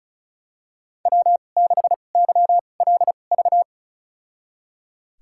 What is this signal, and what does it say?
Morse code, heard as a keyed tone, about 36 wpm, spaced as for 25 wpm, 700 Hz. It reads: W6YLV
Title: USA Call Signs - 2x Character Spacing - 35wpm